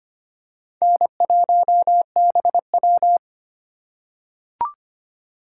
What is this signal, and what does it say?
Morse code, heard as a keyed tone, about 25 wpm, 700 Hz. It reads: N1BW
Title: USA Call Signs - 4 Characters - 2x Character Spacing - 25wpm